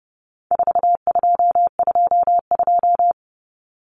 Morse code, heard as a keyed tone, about 30 wpm, 700 Hz. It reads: 4222
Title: Common Cut Numbers - 4-Digit 30wpm